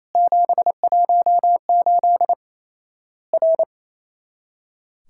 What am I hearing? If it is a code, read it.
Morse code, heard as a keyed tone, about 28 wpm, 700 Hz. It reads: 718 R
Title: Worked All Europe DX Contest - QTC Exchange - 28wpm